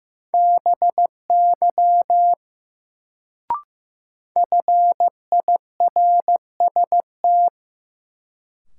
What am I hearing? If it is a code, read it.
Morse code, heard as a keyed tone, about 15 wpm, 700 Hz. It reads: BY FIRST
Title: Sets of 2 Words - New Word Drill - 15wpm